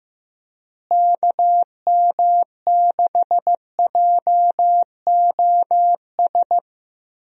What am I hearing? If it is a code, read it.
Morse code, heard as a keyed tone, about 16 wpm, 700 Hz. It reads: KM6JOS
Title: USA Call Signs - 6 Characters - Repeated 3x - 15wpm